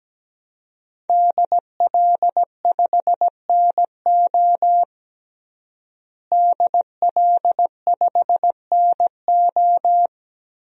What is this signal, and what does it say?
Morse code, heard as a keyed tone, about 17 wpm, 700 Hz. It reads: DL5NO DL5NO
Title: DXCC-100 Call Signs - 5 Characters - Repeated 3x - 17wpm